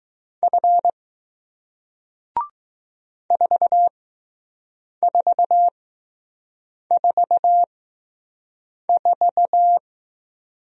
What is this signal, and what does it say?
Morse code, heard as a keyed tone, about 20 wpm, spaced as for 6 wpm, 700 Hz. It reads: F 4444